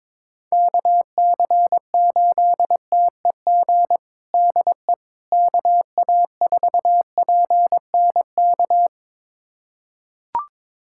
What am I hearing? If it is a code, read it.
Morse code, heard as a keyed tone, about 22 wpm, 700 Hz. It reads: KC8TEG DE KA4PNK
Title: USA Callsign DE Callsign 22wpm